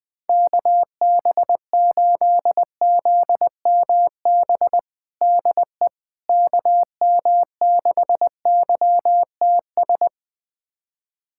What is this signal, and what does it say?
Morse code, heard as a keyed tone, about 20 wpm, 700 Hz. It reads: KB8ZMB DE KM6YTS
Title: USA Callsign DE Callsign - 6 Characters - 20wpm